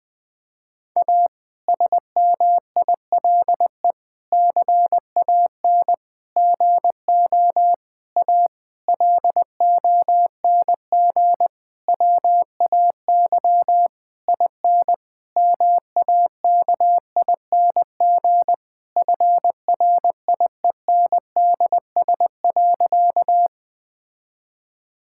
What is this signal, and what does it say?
Morse code, heard as a keyed tone, about 20 wpm, 700 Hz. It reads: A SMILE CAN GO A LONG WAY IN MAKING FRIENDS.
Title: Difficult Sentences with Hints - 20wpm